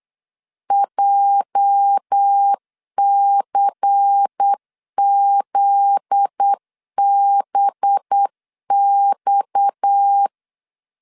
Morse code, heard as a keyed tone, about 8 wpm, 800 Hz. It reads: JCZBX
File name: LEZIONI / 8 YJ45 / 8 YJ 8wpm 3313